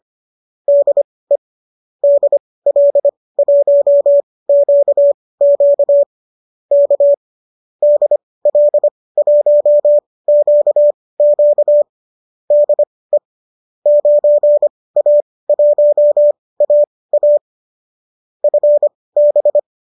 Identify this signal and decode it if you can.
Morse code, heard as a keyed tone, about 25 wpm, spaced as for 18 wpm, 590 Hz. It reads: DE DL1QQ K DL1QQ DE 9A1AA FB